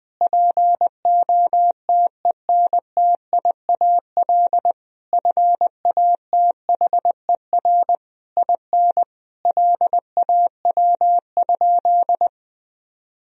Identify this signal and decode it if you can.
Morse code, heard as a keyed tone, about 21 wpm, 700 Hz. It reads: POTENTIAL FATHER IN LAW?
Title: Sentences from Top 1000 Words - Review Part 4 - 20wpm